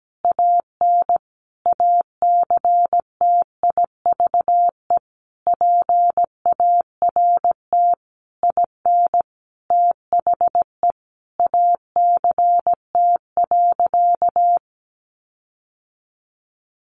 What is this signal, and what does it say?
Morse code, heard as a keyed tone, about 17 wpm, 700 Hz. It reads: AN ACTIVE PART IN THE ACT.
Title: Sentences from Top 1300 Words 17wpm